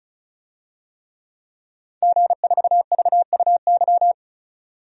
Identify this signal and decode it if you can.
Morse code, heard as a keyed tone, about 34 wpm, 700 Hz. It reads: G4VUY
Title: DXCC-100 Call Signs - 5 Characters - 35wpm